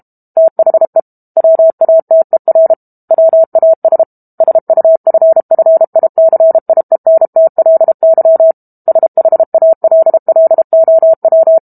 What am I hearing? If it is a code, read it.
Morse code, heard as a keyed tone, about 32 wpm, 650 Hz. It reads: THE WATER WAS SUFFICIENTLY SHALLOW
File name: Book Two / 06-Chapter 6 - 32 WPM